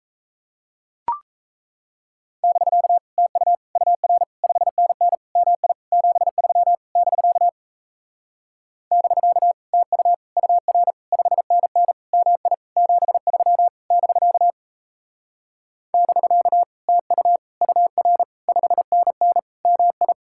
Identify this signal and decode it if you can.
Morse code, heard as a keyed tone, about 38 wpm, 700 Hz. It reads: <BK> TU UR 5NN MI 73 <BK> <BK> TU UR 5NN MI 73 <BK> <BK> TU UR 5NN MI